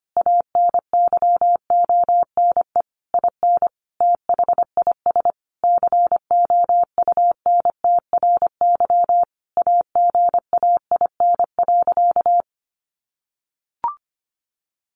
Morse code, heard as a keyed tone, about 25 wpm, 700 Hz. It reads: ANYONE IN THIS COUNTRY AGAIN.
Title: Sentences from Top 1200 Words 25wpm